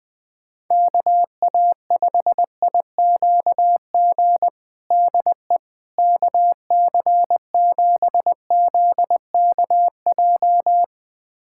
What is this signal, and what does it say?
Morse code, heard as a keyed tone, about 20 wpm, 700 Hz. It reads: KA5IQG DE KC7ZKJ
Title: USA Callsign DE Callsign - 6 Characters - 20wpm